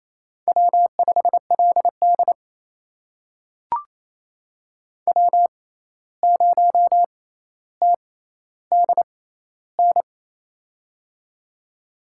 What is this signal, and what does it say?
Morse code, heard as a keyed tone, about 28 wpm, spaced as for 13 wpm, 700 Hz. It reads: W5LD W0TDN
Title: USA Call Signs - 6x Character Spacing - 28wpm